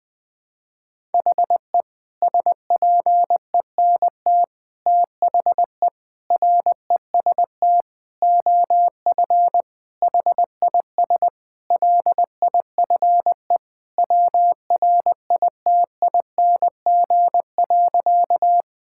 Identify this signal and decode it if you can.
Morse code, heard as a keyed tone, about 21 wpm, 700 Hz. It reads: HE SPENT THE REST OF HIS LIFE WRITING.
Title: Sentences with 5-Word Phrases - 20wpm